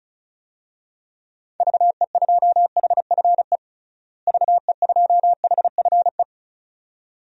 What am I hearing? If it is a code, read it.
Morse code, heard as a keyed tone, about 36 wpm, 700 Hz. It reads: VE2HFE VE2HFE
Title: DXCC-100 Call Signs - 6 Characters - Repeated 2x - 35wpm